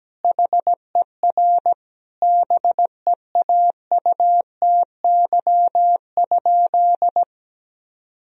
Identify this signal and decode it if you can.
Morse code, heard as a keyed tone, about 17 wpm, 700 Hz. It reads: HER BEAUTY?
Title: Sentences from Top 1000 Words - Review Part 2 - 17wpm